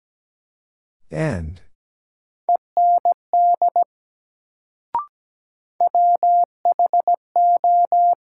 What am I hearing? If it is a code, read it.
Morse code, heard as a keyed tone, about 17 wpm, 700 Hz. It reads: END WHO
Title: Sets of 2 Words - New Word Drill - 17wpm